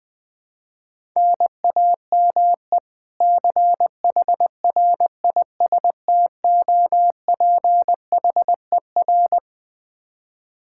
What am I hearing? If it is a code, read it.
Morse code, heard as a keyed tone, about 21 wpm, 700 Hz. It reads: NAME CHRISTOPHER
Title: Practice QSO - Exchange 1 - Beginning - 20wpm